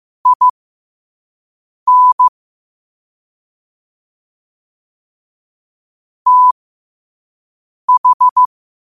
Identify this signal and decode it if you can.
Morse code, heard as a keyed tone, about 15 wpm, spaced as for 5 wpm, 1000 Hz. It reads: IN TH
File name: qotdfiles / 5wpm